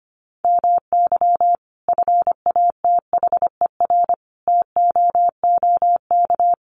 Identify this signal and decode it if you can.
Morse code, heard as a keyed tone, about 25 wpm, 700 Hz. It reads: MY FATHER TOOK
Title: Sentences from Top 1500 Words 25wpm